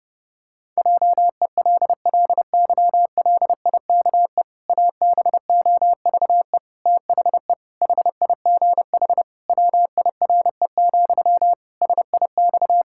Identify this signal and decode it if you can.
Morse code, heard as a keyed tone, about 30 wpm, 700 Hz. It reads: JELLYLIKE ABOVE THE HIGH WIRE, SIX
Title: Pangram Challenge v2 - 30wpm